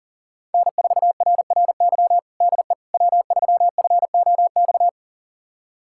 Morse code, heard as a keyed tone, about 37 wpm, 700 Hz. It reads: N4RRY DE W3FOX
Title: USA Callsign DE Callsign - 5 Characters - 40wpm